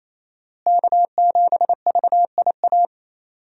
Morse code, heard as a keyed tone, about 28 wpm, 700 Hz. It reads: K7VIA
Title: USA Call Signs - 5 Characters - Repeated 2x - 28wpm